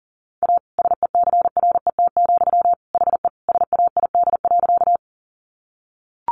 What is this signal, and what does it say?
Morse code, heard as a keyed tone, about 40 wpm, 700 Hz. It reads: A SECRET, HE SAID.
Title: Sentences from Top 1200 Words 40wpm